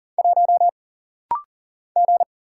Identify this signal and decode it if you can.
Morse code, heard as a keyed tone, about 40 wpm, 700 Hz. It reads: 1 G
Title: Single Letter-Number - Rapid-Fire 40wpm